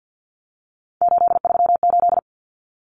Morse code, heard as a keyed tone, about 50 wpm, 700 Hz. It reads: 838
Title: Numbers - 3-Digit 50wpm